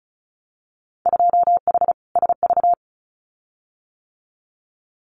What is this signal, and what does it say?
Morse code, heard as a keyed tone, about 36 wpm, 700 Hz. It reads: 2H SV
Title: Winter Field Day Exchanges Encore 35wpm